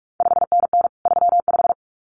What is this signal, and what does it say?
Morse code, heard as a keyed tone, about 46 wpm, 700 Hz. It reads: 5NN 35
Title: CQ WW DX Contest Exchange - 45wpm